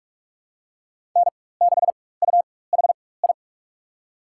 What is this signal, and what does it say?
Morse code, heard as a keyed tone, about 45 wpm, spaced as for 21 wpm, 700 Hz. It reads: N6UHI
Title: USA Call Signs - 5 Characters - 4x Character Spacing - 45wpm